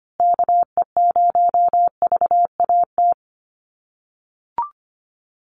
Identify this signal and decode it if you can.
Morse code, heard as a keyed tone, about 25 wpm, 700 Hz. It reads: KE0VAT
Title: USA Call Signs Repeated 2x 25wpm